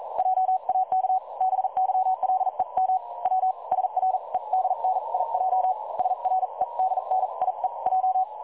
Morse code, heard as a keyed tone, about 43 wpm, spaced as for 36 wpm, 750 Hz. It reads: JAK BYLEM W SREDNIJ SREDNIEJ
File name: dlazaawansowanych / nagraniazpasma / websdr recording 3565.0kHz